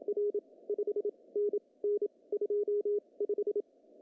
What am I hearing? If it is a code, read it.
Morse code, heard as a keyed tone, about 27 wpm, spaced as for 21 wpm, 410 Hz. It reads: R 5NN 25
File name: CW / cqww-cw-k5tr